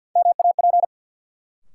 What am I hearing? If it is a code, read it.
Morse code, heard as a keyed tone, about 53 wpm, 700 Hz. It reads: MAP
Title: Sentences from Top 400 Words - New Word Drill - 50wpm